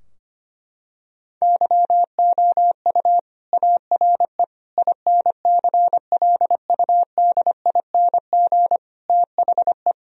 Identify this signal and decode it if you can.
Morse code, heard as a keyed tone, about 25 wpm, 700 Hz. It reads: YOU ARE INCLUDING THE
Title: Sentences from Top 1000 Words - Review Part 4 - 25wpm